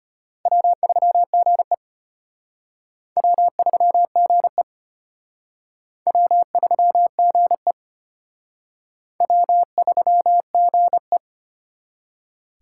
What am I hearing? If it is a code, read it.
Morse code, heard as a keyed tone, about 32 wpm, 700 Hz. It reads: W3GE W3GE W3GE W3GE